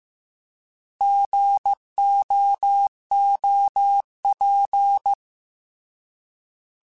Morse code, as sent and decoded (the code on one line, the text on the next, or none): --. --- --- .--.
GOOP